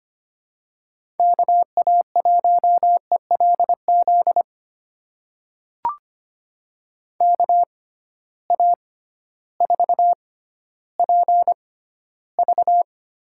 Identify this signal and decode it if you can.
Morse code, heard as a keyed tone, about 25 wpm, spaced as for 13 wpm, 700 Hz. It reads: KA1ELZ KA4PV
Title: USA Call Signs - 6 Characters - 6x Character Spacing - 25wpm